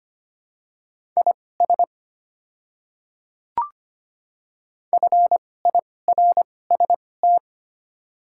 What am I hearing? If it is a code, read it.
Morse code, heard as a keyed tone, about 25 wpm, spaced as for 18 wpm, 690 Hz. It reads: IS FIRST
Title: 2x Character Spacing - Top 100 Words - 25wpm